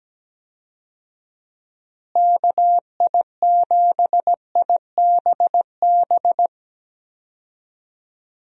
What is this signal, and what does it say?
Morse code, heard as a keyed tone, about 17 wpm, 700 Hz. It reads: KI7IBB
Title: USA Call Signs - 6 Characters - Repeated 2x - 17wpm